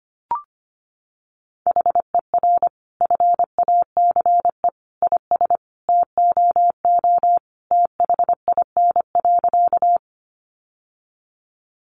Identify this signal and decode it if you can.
Morse code, heard as a keyed tone, about 25 wpm, 700 Hz. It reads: HER FACE IS TOO THIN.
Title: Sentences from Top 1200 Words 25wpm